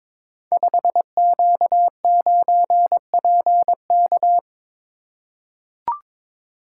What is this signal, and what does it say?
Morse code, heard as a keyed tone, about 22 wpm, 700 Hz. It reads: 5Q9PK
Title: Code Groups - 5 Letter:Numbers - 22wpm